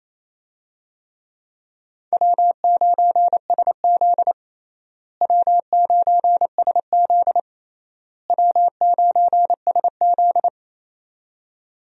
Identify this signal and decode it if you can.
Morse code, heard as a keyed tone, about 28 wpm, 700 Hz. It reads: W9SZ W9SZ W9SZ
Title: USA Call Signs - 4 Characters - Repeated 3x - 28wpm